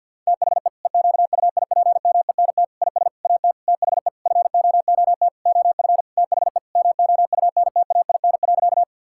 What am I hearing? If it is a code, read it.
Morse code, heard as a keyed tone, about 48 wpm, 700 Hz. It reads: THE EQUIPMENT IS AT THE FOOT OF THE MOUNTAIN.
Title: Sentences from Top 1000 Words - Review Part 5 - 50wpm